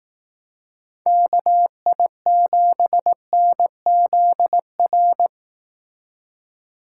Morse code, heard as a keyed tone, about 19 wpm, 700 Hz. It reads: KI7NZR
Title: Speed Racing - USA 2x3 Call Signs 15wpm